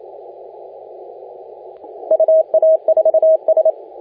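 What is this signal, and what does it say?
Morse code, heard as a keyed tone, about 28 wpm, 640 Hz, with radio noise. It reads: UA4S
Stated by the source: Morse contest audio recorded by LA4C during the SAC CW 2018 contest.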